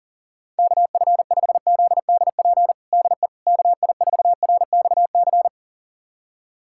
Morse code, heard as a keyed tone, about 37 wpm, 700 Hz. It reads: KF5ZDP DE KI4RXC
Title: USA Callsign DE Callsign 40wpm